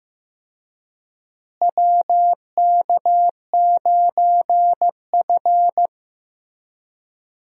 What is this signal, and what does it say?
Morse code, heard as a keyed tone, about 16 wpm, 700 Hz. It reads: WK9F
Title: USA Call Signs - 4 Characters - Repeated 3x - 15wpm